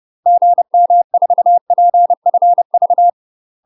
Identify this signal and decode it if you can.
Morse code, heard as a keyed tone, about 30 wpm, 700 Hz. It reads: GM4PFV
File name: UK MASTER QSOs TRAINING / 30WPM / UK MASTER QSOs 0016